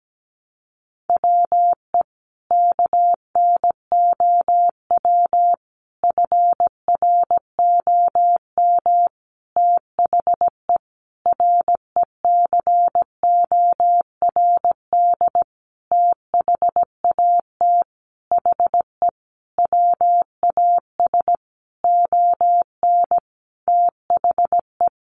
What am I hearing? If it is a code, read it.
Morse code, heard as a keyed tone, about 17 wpm, 700 Hz. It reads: WE KNOW FROM THE RECORD THAT HE WAS ON THE
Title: Sentences from Top 500 Words 17wpm